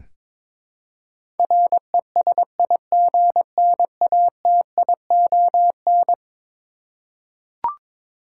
Morse code, heard as a keyed tone, about 22 wpm, 700 Hz. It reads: RESIGNATION
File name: practice / 22wpm / Eleven Letter Words - 22wpm